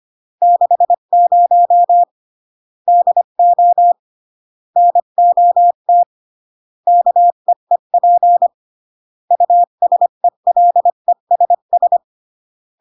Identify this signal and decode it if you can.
Morse code, heard as a keyed tone, about 25 wpm, 700 Hz. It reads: 60 DO NOT KEEP USELESS